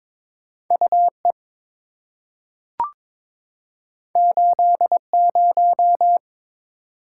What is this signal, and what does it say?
Morse code, heard as a keyed tone, about 23 wpm, 700 Hz. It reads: UE 80
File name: practice / 22wpm / Two Letter-Number Combinations - 22wpm